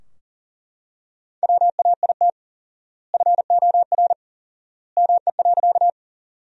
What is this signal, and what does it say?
Morse code, heard as a keyed tone, about 37 wpm, 700 Hz. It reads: WAIT FOR ME.
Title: 4x Word-Spacing - Sentences from Top 500 Words - 40wpm